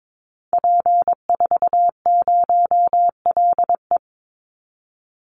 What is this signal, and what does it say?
Morse code, heard as a keyed tone, about 22 wpm, 690 Hz. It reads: P40LE
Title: Call Signs Repeated 2x - 22wpm